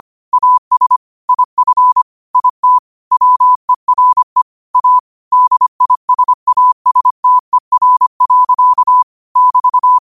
Morse code, heard as a keyed tone, about 25 wpm, 1000 Hz. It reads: AS IF IT WERE A DISASTER. <BT>